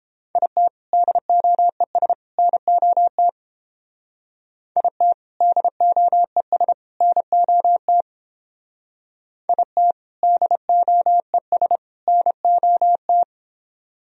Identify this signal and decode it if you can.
Morse code, heard as a keyed tone, about 30 wpm, 700 Hz. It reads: IT DOES NOT IT DOES NOT IT DOES NOT